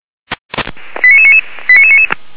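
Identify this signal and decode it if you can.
Five-tone selective call (ZVEI), heard as quick runs of five tones, tones of about 70 ms. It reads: 89999 89999
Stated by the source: ZVEI five-tone call 89999 received on SDR.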